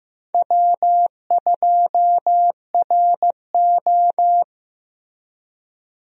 Morse code, heard as a keyed tone, about 15 wpm, 700 Hz. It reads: W2RO
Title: USA Call Signs - 4 Characters - Repeated 3x - 15wpm